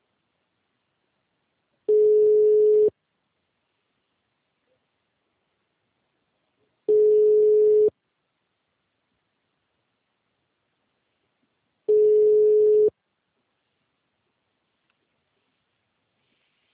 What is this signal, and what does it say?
Ringback tone (Europe, 425 Hz).